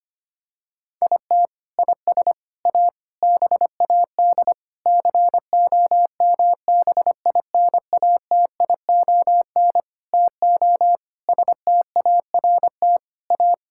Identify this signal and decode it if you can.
Morse code, heard as a keyed tone, about 25 wpm, 700 Hz. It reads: IT IS A BAD COMBINATION TO START A
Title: Sentences from Top 1000 Words - Review Part 6 - 25wpm